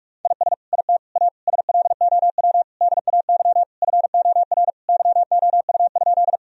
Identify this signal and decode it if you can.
Morse code, heard as a keyed tone, about 46 wpm, 700 Hz. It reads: IS IT A SLOW DAY FOR YOU?
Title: Sentences from Top 1000 Words - Review Part 2 - 45wpm